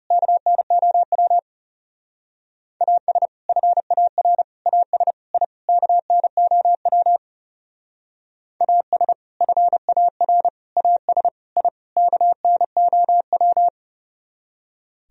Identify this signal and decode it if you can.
Morse code, heard as a keyed tone, about 36 wpm, 700 Hz. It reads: KNOW AS FAR AS I KNOW AS FAR AS I KNOW
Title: Speed-Racing - Sets of 5 Words 30wpm